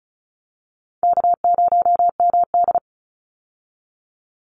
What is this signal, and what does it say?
Morse code, heard as a keyed tone, about 34 wpm, 700 Hz. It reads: K0MD
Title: Call Signs Repeated 2x - 35wpm